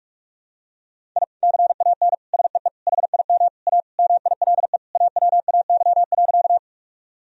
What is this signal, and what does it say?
Morse code, heard as a keyed tone, about 41 wpm, 700 Hz. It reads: I CAN SEE HIM A MILE AWAY.
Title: Sentences from Top 1000 Words - Review Part 1 - 45wpm